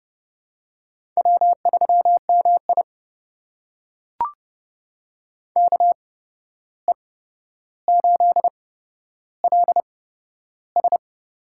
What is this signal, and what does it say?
Morse code, heard as a keyed tone, about 30 wpm, spaced as for 12 wpm, 700 Hz. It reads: W3MI KE8LS
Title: USA Call Signs - 8x Character Spacing - 30wpm